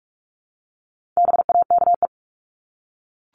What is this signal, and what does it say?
Morse code, heard as a keyed tone, about 44 wpm, 700 Hz. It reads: BAKE
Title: Top 1751-2000 Words 45wpm